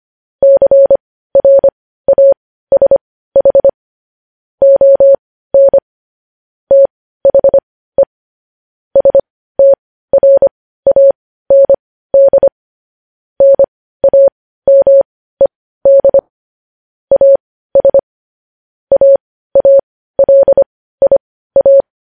Morse code, heard as a keyed tone, about 25 wpm, spaced as for 15 wpm, 550 Hz. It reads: CRASH ON THE STRAND NAMED AS AALIA
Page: news headlines in morse code at 15 wpm